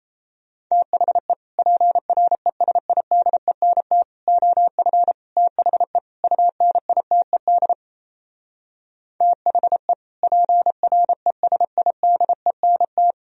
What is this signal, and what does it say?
Morse code, heard as a keyed tone, about 38 wpm, 700 Hz. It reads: THE PRESIDENT OF THE UNITED THE PRESIDENT